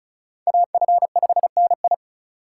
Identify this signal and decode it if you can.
Morse code, heard as a keyed tone, about 43 wpm, 700 Hz. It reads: AF5NI
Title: Speed Racing - USA 2x2 Call Signs 35wpm